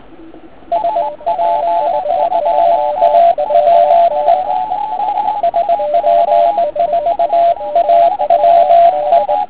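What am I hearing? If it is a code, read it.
Morse code, heard as a keyed tone, about 19 wpm, 680 Hz. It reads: SP4AWE SP4AWE
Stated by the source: Morning 80m pile-up recorded at nearly the same time on three transceivers: FT1000MKV, FT817 and ATS-3B.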